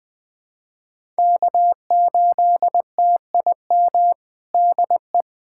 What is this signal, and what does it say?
Morse code, heard as a keyed tone, about 19 wpm, 700 Hz. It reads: K8TIM DE
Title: USA Callsign DE Callsign - 5 Characters - 20wpm